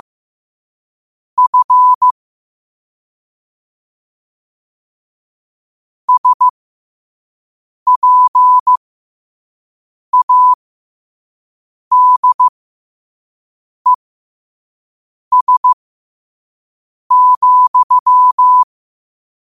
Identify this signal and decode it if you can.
Morse code, heard as a keyed tone, about 15 wpm, spaced as for 5 wpm, 1000 Hz. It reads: F SPADES,
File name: qotdfiles / 5wpm